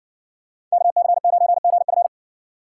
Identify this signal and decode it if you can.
Morse code, heard as a keyed tone, about 55 wpm, 700 Hz. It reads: KC9GF